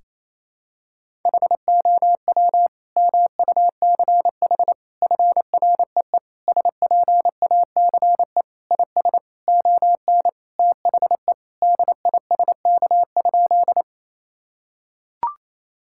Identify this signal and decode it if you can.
Morse code, heard as a keyed tone, about 28 wpm, 700 Hz. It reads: HOW MUCH FREE SPACE IS ON THE DISK?
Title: Sentences from Top 1000 Words - Review Part 6 - 28wpm